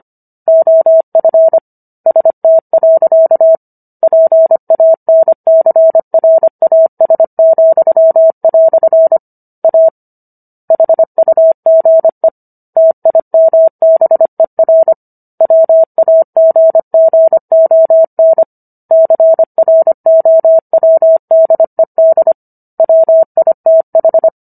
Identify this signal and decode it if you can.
Morse code, heard as a keyed tone, about 25 wpm, 650 Hz. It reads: OF ST. PANCRAS," A HUGE TIMBER WAGGON CROWDED WITH